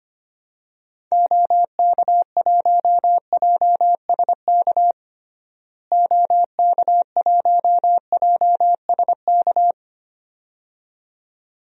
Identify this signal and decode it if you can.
Morse code, heard as a keyed tone, about 25 wpm, 700 Hz. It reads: OK1JSK OK1JSK
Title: DXCC-100 Call Signs - 6 Characters - Repeated 2x - 25wpm